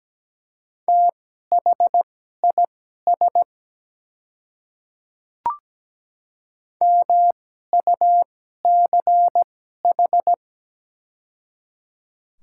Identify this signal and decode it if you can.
Morse code, heard as a keyed tone, about 17 wpm, spaced as for 12 wpm, 700 Hz. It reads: THIS MUCH
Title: 2x Character Spacing - Top 300 Words - 17wpm